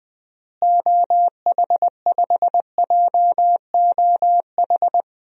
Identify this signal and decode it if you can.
Morse code, heard as a keyed tone, about 20 wpm, 690 Hz. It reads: OH5JOH
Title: DXCC-100 Call Signs - 6 Characters - Repeated 2x - 20wpm